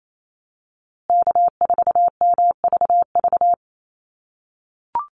Morse code, heard as a keyed tone, about 28 wpm, 700 Hz. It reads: K4MVV
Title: USA Call Signs Repeated 3x 28wpm